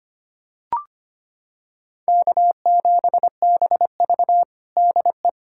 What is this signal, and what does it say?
Morse code, heard as a keyed tone, about 25 wpm, 700 Hz. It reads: K7BV DE
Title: Practice QSO - Exchange 4 - Ending - 25wpm